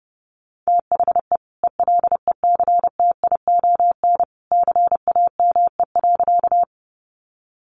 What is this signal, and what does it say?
Morse code, heard as a keyed tone, about 30 wpm, 700 Hz. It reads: THE ELECTION CAME.
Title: Sentences from Top 1500 Words 30wpm